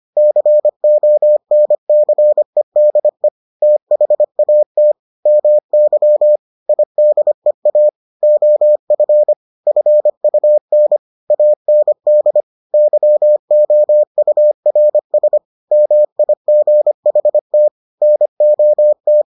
Morse code, heard as a keyed tone, about 25 wpm, 600 Hz. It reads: CONCEDE THAT MY IDEA OF FUN AND YOURS MIGHT NOT